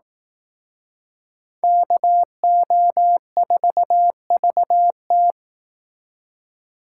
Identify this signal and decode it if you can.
Morse code, heard as a keyed tone, about 18 wpm, 700 Hz. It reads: KO4VT
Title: Speed Racing - USA 2x2 Call Signs 15wpm